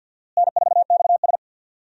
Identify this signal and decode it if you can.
Morse code, heard as a keyed tone, about 46 wpm, 700 Hz. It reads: N4XS
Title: USA Call Signs - 4 Characters - Repeated 2x - 50wpm